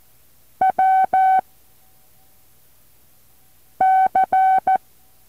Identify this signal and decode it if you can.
Morse code, heard as a keyed tone, about 14 wpm, 760 Hz. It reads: W C
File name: intl morse code2